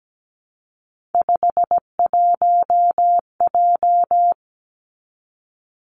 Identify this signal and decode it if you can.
Morse code, heard as a keyed tone, about 17 wpm, 700 Hz. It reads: 51J
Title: Three Challenging Letter-Number Combinations 17wpm